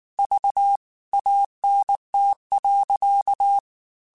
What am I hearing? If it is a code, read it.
Morse code, heard as a keyed tone, about 19 wpm, 800 Hz. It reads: V ANT.
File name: QSOs / QSOs-mp3s / WD0FBT 1@18